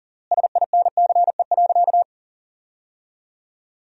Morse code, heard as a keyed tone, about 40 wpm, 700 Hz. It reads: SINCE.
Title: 3x Word-Spacing - Sentences from Top 400 Words - 40wpm